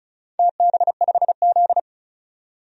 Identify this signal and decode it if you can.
Morse code, heard as a keyed tone, about 36 wpm, 700 Hz. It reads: TB5Z
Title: DXCC-100 Call Signs - 4 Characters - 35wpm